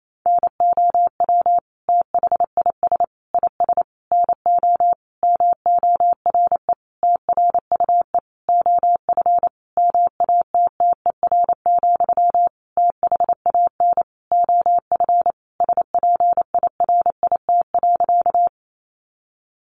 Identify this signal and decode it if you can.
Morse code, heard as a keyed tone, about 28 wpm, 700 Hz. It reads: NOW THIS IS NO MORE TRUE OF MATTER, THAN OF SPIRIT.
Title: Sentences from Top 1400 Words 28wpm